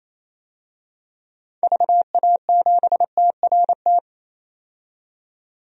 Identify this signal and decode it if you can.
Morse code, heard as a keyed tone, about 28 wpm, 700 Hz. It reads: VA7TRT
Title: DXCC-100 Call Signs - 6 Characters - 28wpm